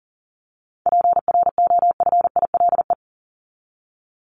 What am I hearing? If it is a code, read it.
Morse code, heard as a keyed tone, about 40 wpm, 700 Hz. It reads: PROFILE
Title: Top 1501-1750 Words 40wpm